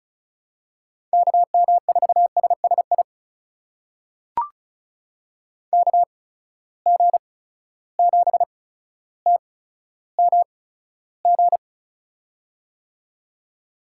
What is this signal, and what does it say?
Morse code, heard as a keyed tone, about 36 wpm, spaced as for 15 wpm, 700 Hz. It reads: KM4SSI KG7TMG
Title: USA Call Signs - 6 Characters - 8x Character Spacing - 35wpm